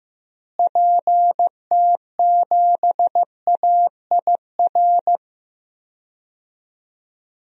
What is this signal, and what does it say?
Morse code, heard as a keyed tone, about 15 wpm, 700 Hz. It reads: PT7AIR
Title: DXCC-100 Call Signs - 6 Characters - Repeated 2x - 15wpm